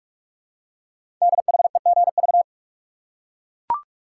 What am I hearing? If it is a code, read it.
Morse code, heard as a keyed tone, about 44 wpm, 700 Hz. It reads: DHEGV
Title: Code Groups - 5 Letters - 45wpm